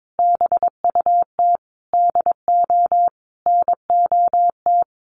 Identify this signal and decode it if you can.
Morse code, heard as a keyed tone, about 22 wpm, 700 Hz. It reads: BUT DO NOT
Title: Sentences from Top 1500 Words 22wpm